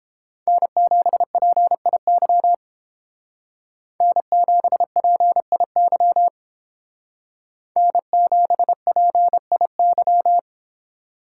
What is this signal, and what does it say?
Morse code, heard as a keyed tone, about 29 wpm, 700 Hz. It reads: N7PIY N7PIY N7PIY